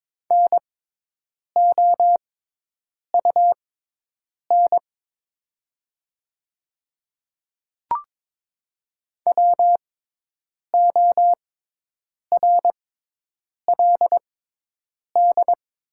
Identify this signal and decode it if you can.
Morse code, heard as a keyed tone, about 22 wpm, spaced as for 8 wpm, 700 Hz. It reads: NOUN WORLD
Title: 6x Character Spacing - Top 500 Words - 22wpm